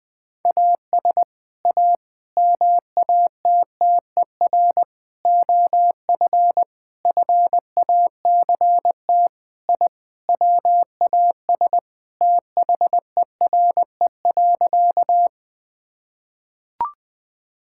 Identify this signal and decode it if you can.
Morse code, heard as a keyed tone, about 20 wpm, 700 Hz. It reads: AS A MATTER OF FACT I WAS THERE.
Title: Sentences with 5-Word Phrases - Spoken Beginning - 20wpm